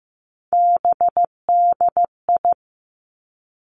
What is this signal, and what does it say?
Morse code, heard as a keyed tone, about 15 wpm, 700 Hz. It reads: BDI
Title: Three Challenging Letter-Number Combinations 15wpm